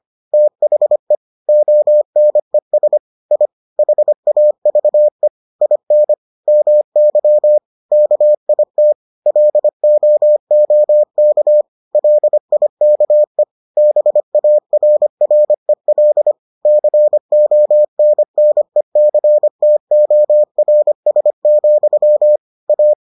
Morse code, heard as a keyed tone, about 25 wpm, 600 Hz. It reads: THE ONES I HAVE IN MY KIT LOOK LIKE BARREL CONNECTORS, A